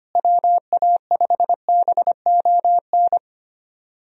Morse code, heard as a keyed tone, about 25 wpm, 700 Hz. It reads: WA5BON
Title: USA Call Signs - 6 Characters - Repeated 2x - 25wpm